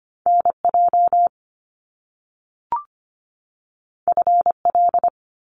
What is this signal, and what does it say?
Morse code, heard as a keyed tone, about 25 wpm, 700 Hz. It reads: NJ FL
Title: US States 25wpm